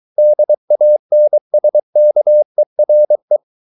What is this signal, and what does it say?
Morse code, heard as a keyed tone, about 23 wpm, 600 Hz. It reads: DANSKERE
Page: Ferie 23 wpm